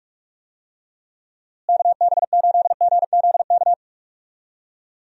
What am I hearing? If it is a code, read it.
Morse code, heard as a keyed tone, about 46 wpm, 700 Hz. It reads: KB8GZK
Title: USA Call Signs - 6 Characters - Repeated 2x - 45wpm